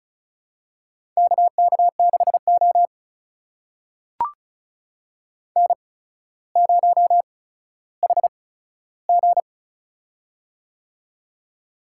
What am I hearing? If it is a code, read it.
Morse code, heard as a keyed tone, about 36 wpm, spaced as for 15 wpm, 700 Hz. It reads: KK6O N0HG
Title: USA Call Signs - 4 Characters - 8x Character Spacing - 35wpm